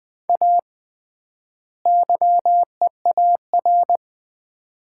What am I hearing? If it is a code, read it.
Morse code, heard as a keyed tone, about 20 wpm, 700 Hz. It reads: A YEAR
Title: 3x Word-Spacing - Sets of 2 Words - 20wpm